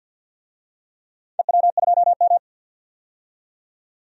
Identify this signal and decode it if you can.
Morse code, heard as a keyed tone, about 53 wpm, 700 Hz. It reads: EW2M
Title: DXCC-100 Call Signs - 4 Characters - 50wpm